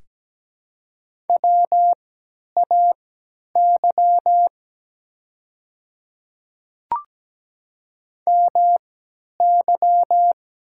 Morse code, heard as a keyed tone, about 17 wpm, spaced as for 10 wpm, 700 Hz. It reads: WAY MY
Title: 3x Character Spacing - Top 100 Words - 17wpm